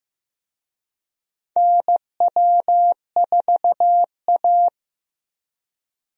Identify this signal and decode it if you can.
Morse code, heard as a keyed tone, about 15 wpm, 700 Hz. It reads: NW4A
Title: USA Call Signs - 4 Characters - Repeated 3x - 15wpm